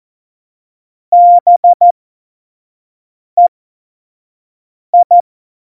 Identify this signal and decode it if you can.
Morse code, heard as a keyed tone, about 14 wpm, spaced as for 5 wpm, 700 Hz. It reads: BEI
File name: Morsekurs / Witze / Witz 12. November 2019, 25 F70